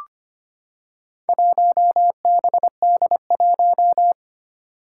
Morse code, heard as a keyed tone, about 25 wpm, 700 Hz. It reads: 1BD1
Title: Four Challenging Letter-Number Combinations 25wpm